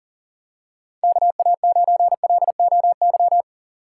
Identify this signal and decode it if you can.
Morse code, heard as a keyed tone, about 40 wpm, 700 Hz. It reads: KA9LOY
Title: USA Call Signs - 2x Character Spacing - 40wpm